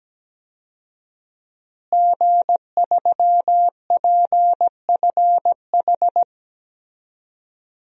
Morse code, heard as a keyed tone, about 17 wpm, 700 Hz. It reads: G3PFH
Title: DXCC-100 Call Signs - 5 Characters - Repeated 3x - 17wpm